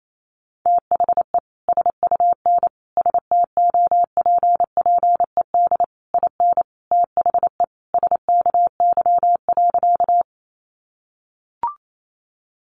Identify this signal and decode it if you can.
Morse code, heard as a keyed tone, about 28 wpm, 700 Hz. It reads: THE SUN STOPPED IN THE SKY.
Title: A Sound Of Thunder 28wpm